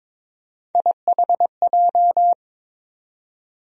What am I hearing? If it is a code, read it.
Morse code, heard as a keyed tone, about 22 wpm, 700 Hz. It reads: IHJ
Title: USA Call Sign Postfixes - 3 Letters 22wpm